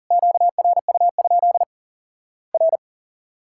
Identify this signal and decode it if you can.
Morse code, heard as a keyed tone, about 40 wpm, 700 Hz. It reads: QRU? R
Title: Worked All Europe DX Contest - QTC Exchange - 40wpm